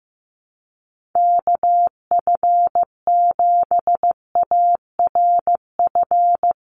Morse code, heard as a keyed tone, about 15 wpm, 700 Hz. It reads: KF7ARF
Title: USA Call Signs 15wpm